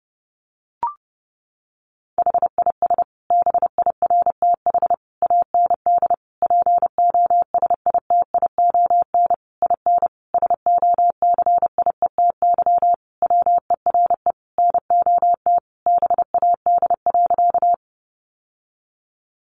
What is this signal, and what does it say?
Morse code, heard as a keyed tone, about 30 wpm, 700 Hz. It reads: HIS BIRTH AND POSITION IN SOCIETY WERE NOT BAD.
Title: Sentences from Top 900 Words 30wpm